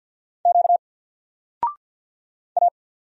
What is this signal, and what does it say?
Morse code, heard as a keyed tone, about 50 wpm, 700 Hz. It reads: Q A
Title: Single Letter-Number 50wpm